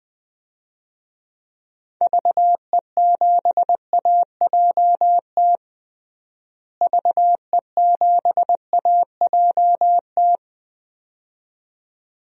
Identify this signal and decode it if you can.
Morse code, heard as a keyed tone, about 20 wpm, 700 Hz. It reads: VE7AJT VE7AJT
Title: DXCC-100 Call Signs - 6 Characters - Repeated 2x - 20wpm